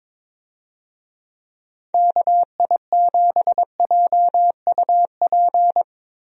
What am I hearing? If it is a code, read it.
Morse code, heard as a keyed tone, about 22 wpm, 700 Hz. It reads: KI7JUP
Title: USA Call Signs - 6 Characters - Repeated 2x - 22wpm